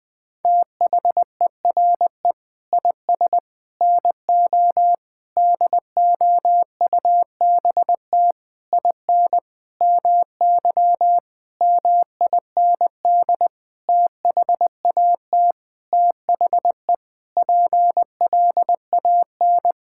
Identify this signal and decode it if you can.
Morse code, heard as a keyed tone, about 20 wpm, 700 Hz. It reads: THERE IS NO DOUBT IN MY MIND THAT THE PLAN
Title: Sentences with 9-Word Phrases - 20wpm